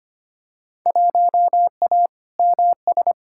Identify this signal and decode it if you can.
Morse code, heard as a keyed tone, about 25 wpm, 700 Hz. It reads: 1A MS
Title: Field Day Exchanges - 25wpm